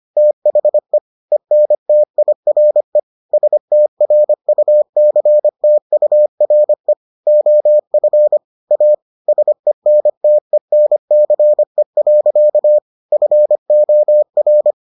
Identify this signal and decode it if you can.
Morse code, heard as a keyed tone, about 25 wpm, 600 Hz. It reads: THE ENTIRE STRUCTURE OF A SENTENCE. FOR